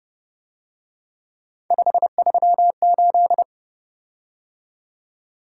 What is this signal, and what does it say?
Morse code, heard as a keyed tone, about 29 wpm, 700 Hz. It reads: 538
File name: Speed-Racing - Three Letter-Number Combinations - 25wpm